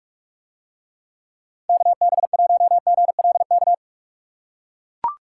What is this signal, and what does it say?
Morse code, heard as a keyed tone, about 44 wpm, 700 Hz. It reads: KB1GLK